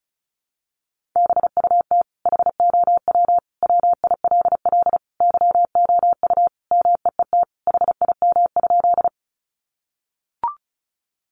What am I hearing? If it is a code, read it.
Morse code, heard as a keyed tone, about 34 wpm, 700 Hz. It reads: BUT HOW WILL YOU MEET HIM?
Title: Sentences from Top 1100 Words 35wpm